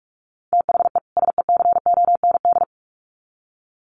Morse code, heard as a keyed tone, about 44 wpm, 700 Hz. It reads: THE SECOND
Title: Sets of 2 Words 45wpm